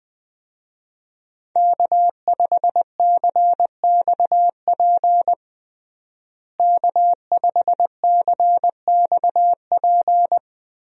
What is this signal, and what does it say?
Morse code, heard as a keyed tone, about 21 wpm, 700 Hz. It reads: K5CXP K5CXP
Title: USA Call Signs - 5 Characters - Repeated 3x - 20wpm